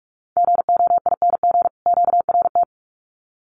Morse code, heard as a keyed tone, about 44 wpm, 700 Hz. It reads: GOING QRT
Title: Common QSO Phrases 45wpm